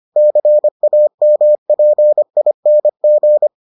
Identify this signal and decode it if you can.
Morse code, heard as a keyed tone, about 25 wpm, 600 Hz. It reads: CAMPING